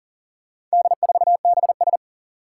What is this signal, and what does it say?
Morse code, heard as a keyed tone, about 40 wpm, 700 Hz. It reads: D4BS
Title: Four Challenging Letter-Number Combinations 40wpm